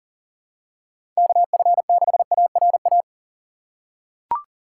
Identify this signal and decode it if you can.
Morse code, heard as a keyed tone, about 40 wpm, 700 Hz. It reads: KF6ARA
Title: USA Call Signs - 6 Characters - 3x Character Spacing - 40wpm